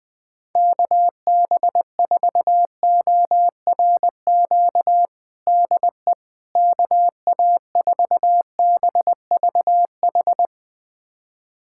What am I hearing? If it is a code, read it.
Morse code, heard as a keyed tone, about 21 wpm, 700 Hz. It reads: KB4ORQ DE KA4BVH
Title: USA Callsign DE Callsign - 6 Characters - 20wpm